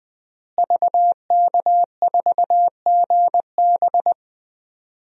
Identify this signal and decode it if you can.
Morse code, heard as a keyed tone, about 20 wpm, 700 Hz. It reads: VK4GB
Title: DXCC-100 Call Signs - 5 Characters - Repeated 2x - 20wpm